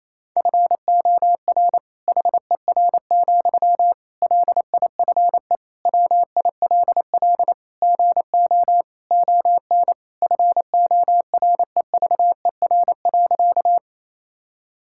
Morse code, heard as a keyed tone, about 27 wpm, 700 Hz. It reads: FOR HER, LIFE WILL GO ON FOREVER.
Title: Sentences from Top 1600 Words 28wpm